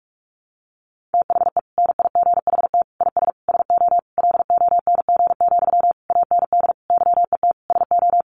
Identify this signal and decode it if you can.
Morse code, heard as a keyed tone, about 44 wpm, 700 Hz. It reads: THE NIGHT IS SO LONG, AND YET SO
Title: Sentences from Top 500 Words 45wpm